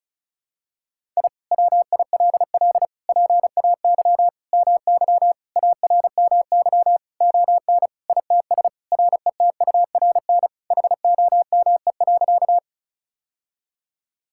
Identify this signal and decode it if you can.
Morse code, heard as a keyed tone, about 34 wpm, 700 Hz. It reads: I WILL PAY MY ARMY ON ITS RETURN HOME.
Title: Sentences from Top 1000 Words - Review Part 5 - 35wpm